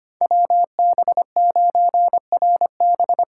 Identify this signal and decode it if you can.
Morse code, heard as a keyed tone, about 25 wpm, 700 Hz. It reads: WB9RB
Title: USA Call Signs - 6 Characters - 6x Character Spacing - 25wpm